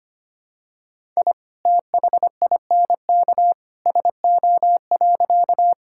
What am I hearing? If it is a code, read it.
Morse code, heard as a keyed tone, about 25 wpm, 690 Hz. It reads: I THINK SO.
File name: practice / 25wpm / Sentences with 2-Word Phrases - 25wpm